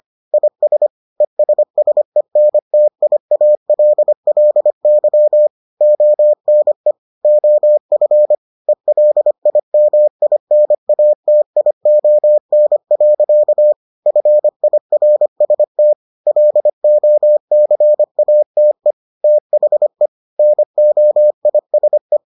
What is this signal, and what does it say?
Morse code, heard as a keyed tone, about 25 wpm, 600 Hz. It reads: IS ESSENTIALLY ONE OF ELIMINATION. FIRST LOCATE THE NOISE